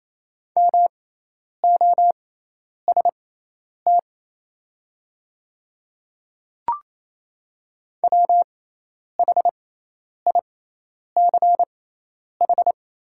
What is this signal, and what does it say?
Morse code, heard as a keyed tone, about 28 wpm, spaced as for 10 wpm, 700 Hz. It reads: MOST WHICH
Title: 6x Character Spacing - Top 200 Words - 28wpm